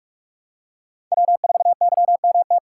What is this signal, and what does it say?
Morse code, heard as a keyed tone, about 41 wpm, 700 Hz. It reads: W4YMT
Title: USA Call Signs - 5 Characters - 2x Character Spacing - 45wpm